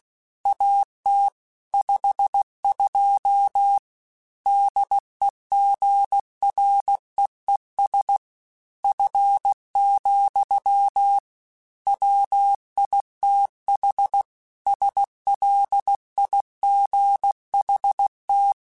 Morse code, heard as a keyed tone, about 16 wpm, 800 Hz. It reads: AT 52 DEGREES F, WITH SLIGHT
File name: Weather Sets / Weather Sets-mp3s / Weather 4 17 @15